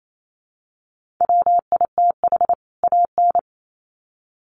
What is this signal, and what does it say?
Morse code, heard as a keyed tone, about 28 wpm, 700 Hz. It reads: WITH AN
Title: Sets of 2 Words 28wpm